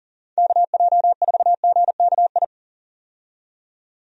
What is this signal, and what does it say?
Morse code, heard as a keyed tone, about 40 wpm, 700 Hz. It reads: KJ4GKI
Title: USA Call Signs - 6 Characters - 40wpm